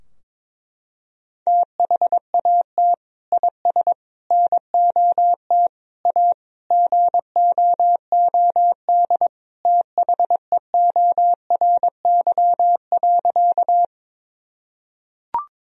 Morse code, heard as a keyed tone, about 22 wpm, 700 Hz. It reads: THAT IS NOT A GOOD THEORY.
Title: Sentences from Top 1000 Words - Review Part 3 - 22wpm